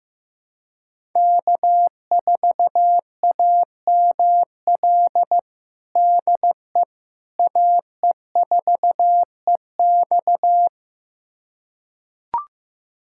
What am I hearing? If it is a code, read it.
Morse code, heard as a keyed tone, about 15 wpm, 700 Hz. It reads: K4AML DE AE4EX
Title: USA Callsign DE Callsign - 5 Characters - 15wpm